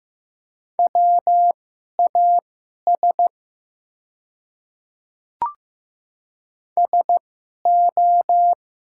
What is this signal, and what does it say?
Morse code, heard as a keyed tone, about 15 wpm, spaced as for 11 wpm, 690 Hz. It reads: WAS SO
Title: 2x Character Spacing - Top 100 Words - 15wpm